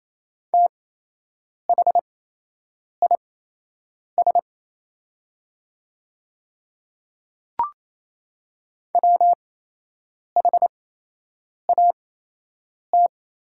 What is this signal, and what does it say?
Morse code, heard as a keyed tone, about 28 wpm, spaced as for 8 wpm, 700 Hz. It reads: THIS WHAT
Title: 8x Character Spacing - Top 100 Words - 28wpm